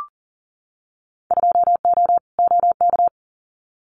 Morse code, heard as a keyed tone, about 40 wpm, 700 Hz. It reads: 2O OK
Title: Winter Field Day Exchanges Encore 40wpm